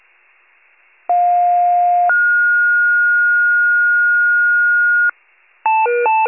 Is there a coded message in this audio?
Yes: two-tone page from 1.1 to 5.1 s, heard as a one-second tone, then a three-second tone, about 710 then 1430 Hz.